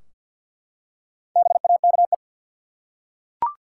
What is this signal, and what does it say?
Morse code, heard as a keyed tone, about 50 wpm, 700 Hz. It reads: BAKE
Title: Four Letter Words - 50wpm